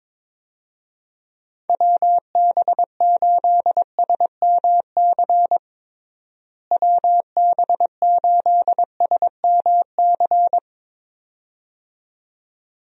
Morse code, heard as a keyed tone, about 21 wpm, 700 Hz. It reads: WB8SMC WB8SMC
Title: USA Call Signs - 6 Characters - Repeated 2x - 22wpm